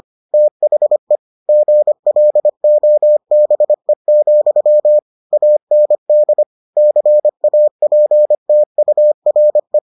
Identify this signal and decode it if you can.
Morse code, heard as a keyed tone, about 25 wpm, 600 Hz. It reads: THE GLOBE, AND CAPTURE